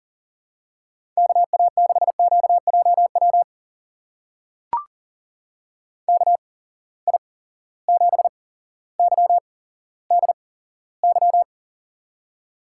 Morse code, heard as a keyed tone, about 39 wpm, spaced as for 17 wpm, 700 Hz. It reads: KA6QJW KI7YDY
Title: USA Call Signs - 6 Characters - 8x Character Spacing - 40wpm